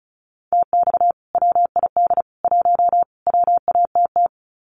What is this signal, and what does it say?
Morse code, heard as a keyed tone, about 34 wpm, 700 Hz. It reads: TX WID 1 WATT
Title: Common QSO Phrases 35wpm